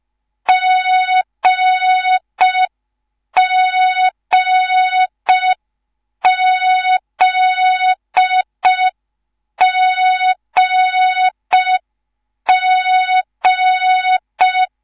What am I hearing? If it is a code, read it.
Morse code, heard as a keyed tone, about 5 wpm, 3700 Hz. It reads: GGZGG